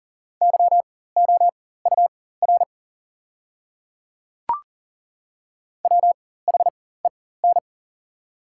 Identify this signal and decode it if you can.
Morse code, heard as a keyed tone, about 40 wpm, spaced as for 19 wpm, 700 Hz. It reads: YOUR WHEN
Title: 4x Character Spacing - Top 200 Words - 40wpm